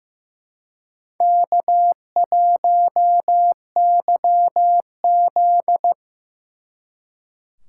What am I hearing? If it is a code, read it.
Morse code, heard as a keyed tone, about 18 wpm, 700 Hz. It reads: K1YZ
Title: Speed Racing - USA 1x2 Call Signs 15wpm